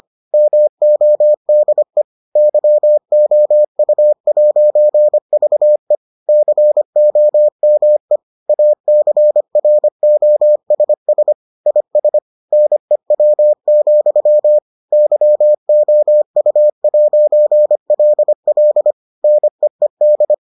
Morse code, heard as a keyed tone, about 25 wpm, 600 Hz. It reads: MODE YOU'VE COME ACROSS IS NEW, YOU'LL NEED